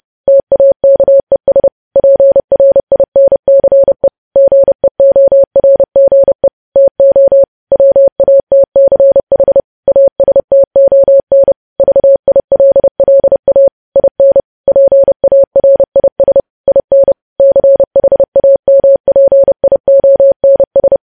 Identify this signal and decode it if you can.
Morse code, heard as a keyed tone, about 30 wpm, 550 Hz. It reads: TAKES PRINCE GEORGE TO WATCH ASTON VILLA IN PARIS IN CHAMPIONS
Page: news headlines in morse code at 30 wpm